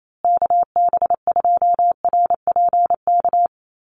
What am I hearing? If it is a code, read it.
Morse code, heard as a keyed tone, about 28 wpm, 700 Hz. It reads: KB2RPK
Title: USA Call Signs Repeated 2x 28wpm